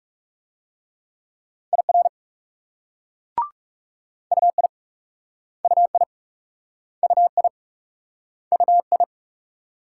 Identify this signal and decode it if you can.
Morse code, heard as a keyed tone, about 38 wpm, 700 Hz. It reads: IR UI UI UI UI